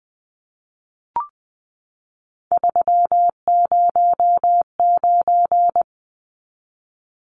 Morse code, heard as a keyed tone, about 20 wpm, 700 Hz. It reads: 309
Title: Numbers - 3-Digit 20wpm